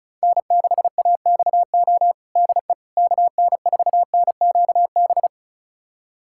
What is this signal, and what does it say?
Morse code, heard as a keyed tone, about 36 wpm, 700 Hz. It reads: N6AXO DE KN4NQB
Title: USA Callsign DE Callsign 35wpm